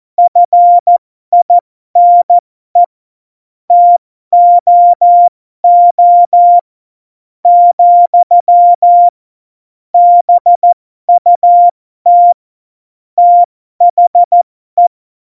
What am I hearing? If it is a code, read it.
Morse code, heard as a keyed tone, about 14 wpm, spaced as for 12 wpm, 700 Hz. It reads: FINE TOO , BUT THE
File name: Morsekurs / Witze